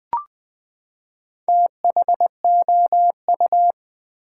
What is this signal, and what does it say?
Morse code, heard as a keyed tone, about 20 wpm, 700 Hz. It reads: THOU
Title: Build A Word - 4 Letters At A Time - 10-Letter Words - 20wpm